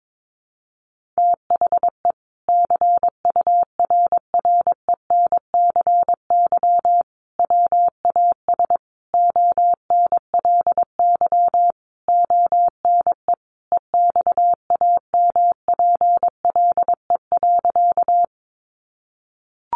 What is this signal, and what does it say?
Morse code, heard as a keyed tone, about 22 wpm, 700 Hz. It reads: THE CURRENCY WAS ONLY ONE EXAMPLE.
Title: Sentences from Top 1500 Words 22wpm